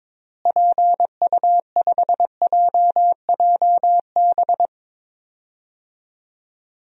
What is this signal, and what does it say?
Morse code, heard as a keyed tone, about 22 wpm, 700 Hz. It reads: PU5JJB
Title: DXCC-100 Call Signs - 6 Characters - Repeated 3x - 22wpm